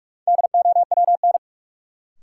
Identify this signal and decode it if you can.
Morse code, heard as a keyed tone, about 46 wpm, 700 Hz. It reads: DOWN
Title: Sentences from Top 300 Words - New Word Drill - 45wpm